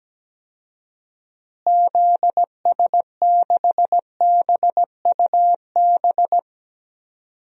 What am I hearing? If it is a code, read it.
Morse code, heard as a keyed tone, about 17 wpm, 700 Hz. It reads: ZS6BUB
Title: DXCC-100 Call Signs - 6 Characters - Repeated 2x - 17wpm